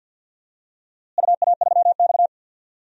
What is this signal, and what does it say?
Morse code, heard as a keyed tone, about 50 wpm, 700 Hz. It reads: UA3X
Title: DXCC-100 Call Signs - 4 Characters - Repeated 2x - 50wpm